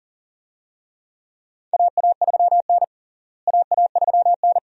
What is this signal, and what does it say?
Morse code, heard as a keyed tone, about 40 wpm, 700 Hz. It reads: AA3N AA3N
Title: USA Call Signs - 4 Characters - Repeated 2x - 40wpm